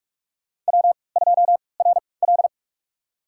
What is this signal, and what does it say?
Morse code, heard as a keyed tone, about 46 wpm, spaced as for 25 wpm, 700 Hz. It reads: W2RL
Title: USA Call Signs - 4 Characters - 3x Character Spacing - 45wpm